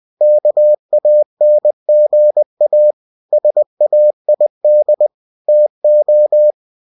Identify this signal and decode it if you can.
Morse code, heard as a keyed tone, about 20 wpm, 600 Hz. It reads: KANGA SAID TO
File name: cw-books / winnie-the-pooh / winnie-20wpm-page-0106